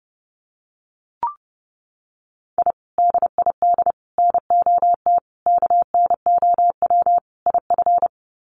Morse code, heard as a keyed tone, about 30 wpm, 700 Hz. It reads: I DID NOT KNOW IF
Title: Sets of 5 Words Encore 30wpm